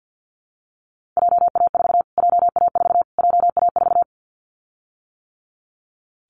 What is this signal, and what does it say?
Morse code, heard as a keyed tone, about 50 wpm, 700 Hz. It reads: JA4 JA4 JA4
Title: Callsign Prefixes for Top 25 DXCC - Repeated 3x - 50wpm